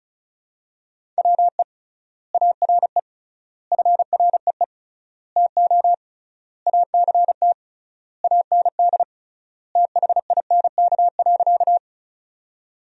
Morse code, heard as a keyed tone, about 34 wpm, 700 Hz. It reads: WE ARE FREE TO ACT AND THINK.
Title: 3x Word-Spacing - Sentences from Top 500 Words - 35wpm